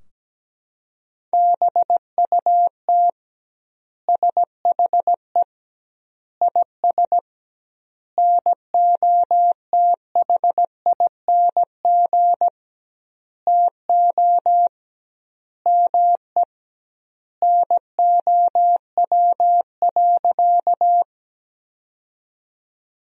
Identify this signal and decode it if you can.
Morse code, heard as a keyed tone, about 17 wpm, 700 Hz. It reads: BUT SHE IS NOTHING TO ME NOW.
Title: 2x Word-Spacing - Sentences from Top 500 Words - 17wpm